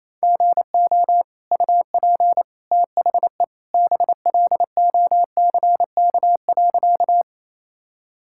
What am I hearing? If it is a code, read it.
Morse code, heard as a keyed tone, about 28 wpm, 700 Hz. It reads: GO UP THE BLOCK.
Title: Sentences from Top 1600 Words 28wpm